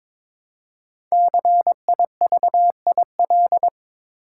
Morse code, heard as a keyed tone, about 22 wpm, 700 Hz. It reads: CIVIL
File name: 5 Letter Words - First 3 Letters Given - 22wpm